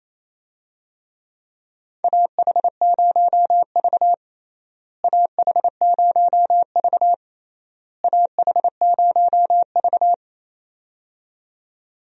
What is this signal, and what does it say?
Morse code, heard as a keyed tone, about 28 wpm, 700 Hz. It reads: AH0V AH0V AH0V
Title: USA Call Signs - 4 Characters - Repeated 3x - 28wpm